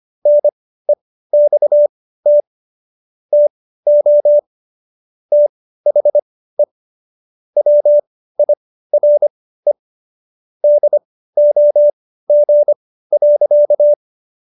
Morse code, heard as a keyed tone, about 25 wpm, spaced as for 15 wpm, 590 Hz. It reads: NEXT TO THE WIRE DOG.